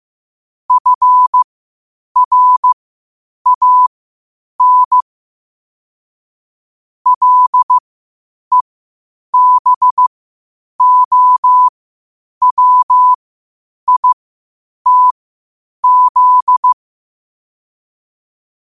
Morse code, heard as a keyed tone, about 15 wpm, spaced as for 8 wpm, 1000 Hz. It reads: FRAN LEBOWITZ
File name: qotdfiles / 8wpm